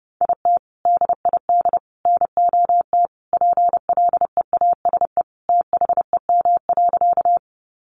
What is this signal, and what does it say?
Morse code, heard as a keyed tone, about 30 wpm, 700 Hz. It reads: IT DID NOT PLEASE THEM.
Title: Sentences from Top 1500 Words 30wpm